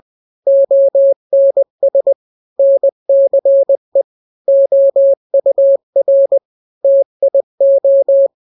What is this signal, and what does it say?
Morse code, heard as a keyed tone, about 20 wpm, 550 Hz. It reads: ONS NCE OUR TIO
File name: n-grams / 3-grams 20 18 WPM